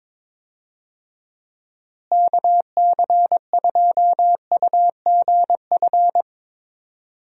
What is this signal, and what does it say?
Morse code, heard as a keyed tone, about 22 wpm, 700 Hz. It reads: KC2UGF
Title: USA Call Signs - 6 Characters - Repeated 2x - 22wpm